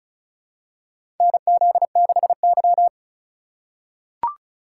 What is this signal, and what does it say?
Morse code, heard as a keyed tone, about 34 wpm, 700 Hz. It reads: NZ6Y
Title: USA Call Signs - 4 Characters - 4x Character Spacing - 35wpm